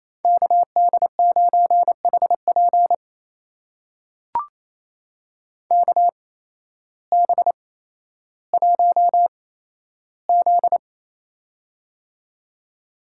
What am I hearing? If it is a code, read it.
Morse code, heard as a keyed tone, about 28 wpm, spaced as for 13 wpm, 700 Hz. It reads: KD9HP KB1Z
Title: USA Call Signs - 8x Character Spacing - 28wpm